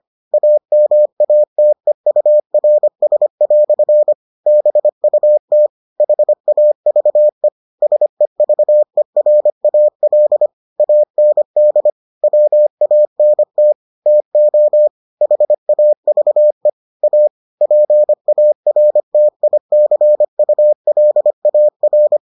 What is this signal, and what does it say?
Morse code, heard as a keyed tone, about 25 wpm, 600 Hz. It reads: AMATEURS" BUT HAVE SEVERAL AND WANT TO HAVE A PARTICULAR